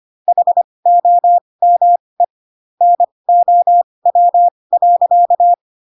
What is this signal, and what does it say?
Morse code, heard as a keyed tone, about 25 wpm, spaced as for 20 wpm, 700 Hz. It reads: HOME NOW.